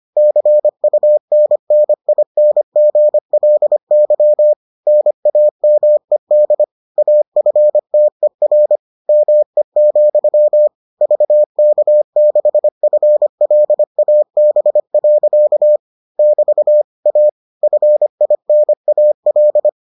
Morse code, heard as a keyed tone, about 25 wpm, 600 Hz. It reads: CUNNINGLY NAMED AFTER ME, VK6FLAB. <BT> A FINAL